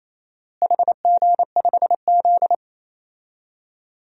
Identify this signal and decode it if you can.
Morse code, heard as a keyed tone, about 28 wpm, 700 Hz. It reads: HG5Z
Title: DXCC-100 Call Signs - 4 Characters - Repeated 2x - 28wpm